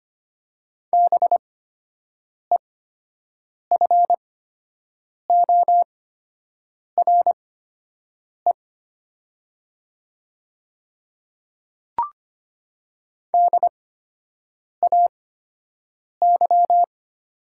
Morse code, heard as a keyed tone, about 25 wpm, spaced as for 7 wpm, 700 Hz. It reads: BEFORE DAY
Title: 8x Character Spacing - Top 200 Words - 25wpm